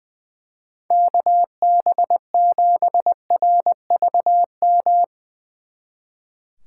Morse code, heard as a keyed tone, about 22 wpm, 700 Hz. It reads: KB7RVM
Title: Speed Racing - USA 2x3 Call Signs 20wpm